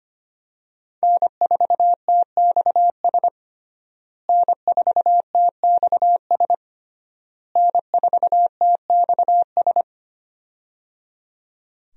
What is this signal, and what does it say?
Morse code, heard as a keyed tone, about 25 wpm, 700 Hz. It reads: N4TXS N4TXS N4TXS
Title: USA Call Signs - 5 Characters - Repeated 3x - 25wpm